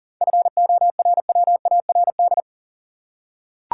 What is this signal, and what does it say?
Morse code, heard as a keyed tone, about 37 wpm, 700 Hz. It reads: FORWARD
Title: Seven Letter Words - 40wpm